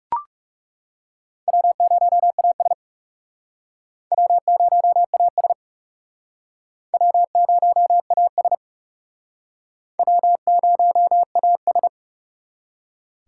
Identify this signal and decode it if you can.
Morse code, heard as a keyed tone, about 40 wpm, 700 Hz. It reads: W0AS W0AS W0AS W0AS